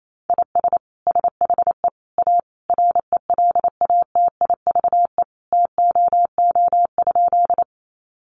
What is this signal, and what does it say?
Morse code, heard as a keyed tone, about 28 wpm, 700 Hz. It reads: IS SHE A RELATIVE TOO?
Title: Sentences from Top 1400 Words 28wpm